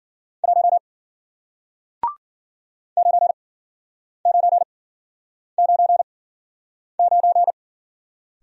Morse code, heard as a keyed tone, about 53 wpm, spaced as for 9 wpm, 700 Hz. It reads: 1 9999